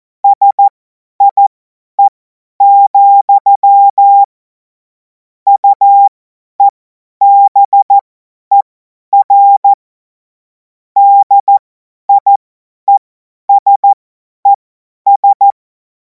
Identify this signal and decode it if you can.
Morse code, heard as a keyed tone, about 14 wpm, spaced as for 10 wpm, 800 Hz. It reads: SIE, UEBER DIESES
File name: Morsekurs / Witze / Witz 17. März 2026 50 F70